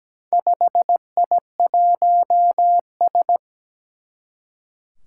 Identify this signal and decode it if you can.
Morse code, heard as a keyed tone, about 17 wpm, 700 Hz. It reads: 5I1S
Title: Four Challenging Letter-Number Combinations 17wpm